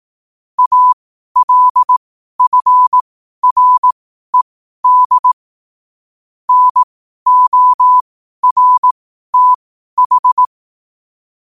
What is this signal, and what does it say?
Morse code, heard as a keyed tone, about 18 wpm, spaced as for 12 wpm, 1000 Hz. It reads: ALFRED NORTH
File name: qotdfiles / 12wpm